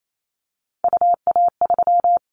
Morse code, heard as a keyed tone, about 28 wpm, 700 Hz. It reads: UA3
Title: Callsign Prefixes for Top 25 DXCC - 28wpm